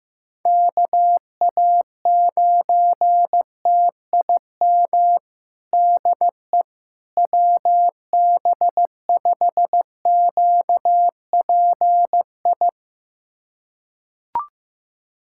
Morse code, heard as a keyed tone, about 15 wpm, 700 Hz. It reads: KA9TIM DE WB5QPI
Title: USA Callsign DE Callsign - 6 Characters - 15wpm